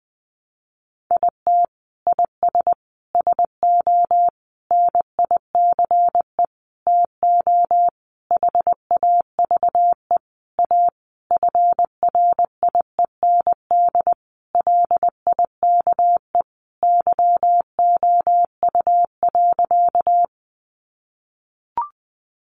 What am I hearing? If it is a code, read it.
Morse code, heard as a keyed tone, about 20 wpm, 700 Hz. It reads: IT IS SO NICE TO HAVE A FRIEND LIKE YOU.
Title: Sentences from Top 1100 Words 20wpm